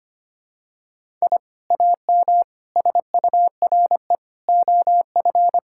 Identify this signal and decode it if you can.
Morse code, heard as a keyed tone, about 25 wpm, 700 Hz. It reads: I AM SURE OF
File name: practice / 25wpm / Sentences with 2-Word Phrases - 25wpm